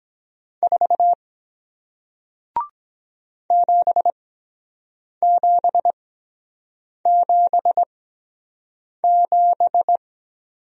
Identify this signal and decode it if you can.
Morse code, heard as a keyed tone, about 23 wpm, spaced as for 7 wpm, 700 Hz. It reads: 4 7777